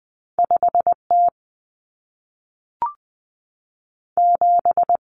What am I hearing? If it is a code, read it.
Morse code, heard as a keyed tone, about 20 wpm, 700 Hz. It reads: 5T 7
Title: Common Cut Numbers - 2-Digit 20wpm